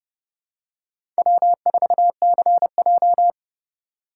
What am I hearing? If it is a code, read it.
Morse code, heard as a keyed tone, about 30 wpm, 700 Hz. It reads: W4CJ
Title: USA Call Signs - 4 Characters - 3x Character Spacing - 30wpm